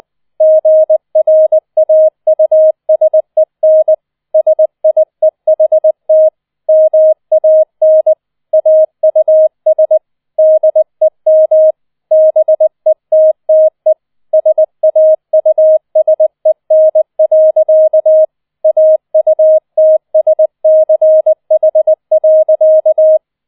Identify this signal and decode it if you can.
Morse code, heard as a keyed tone, about 20 wpm, 630 Hz. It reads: GRAUSEN SIEHT MAN AUS DEM BETTE SAUSEN. AUTSCH.
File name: 20wpm